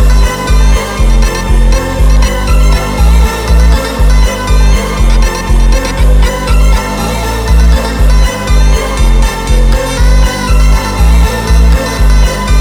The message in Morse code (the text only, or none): none